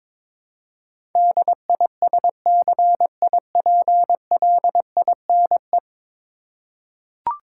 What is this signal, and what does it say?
Morse code, heard as a keyed tone, about 23 wpm, 700 Hz. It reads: DISCIPLINE
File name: practice / 22wpm / 10 Letter Words - First 3 Letters Given - 22wpm